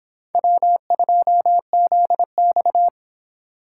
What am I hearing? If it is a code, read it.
Morse code, heard as a keyed tone, about 26 wpm, 700 Hz. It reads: W2ZX